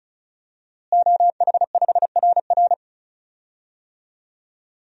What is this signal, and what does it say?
Morse code, heard as a keyed tone, about 36 wpm, 700 Hz. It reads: OH5RR
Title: DXCC-100 Call Signs - 5 Characters - Repeated 3x - 35wpm